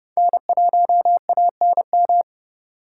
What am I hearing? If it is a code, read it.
Morse code, heard as a keyed tone, about 27 wpm, 700 Hz. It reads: N1ANM